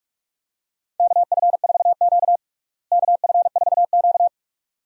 Morse code, heard as a keyed tone, about 41 wpm, 700 Hz. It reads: KF4Q KF4Q
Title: USA Call Signs - 4 Characters - Repeated 2x - 45wpm